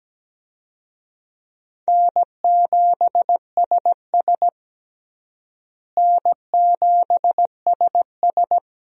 Morse code, heard as a keyed tone, about 17 wpm, 700 Hz. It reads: N7SS N7SS
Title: USA Call Signs - 4 Characters - Repeated 2x - 17wpm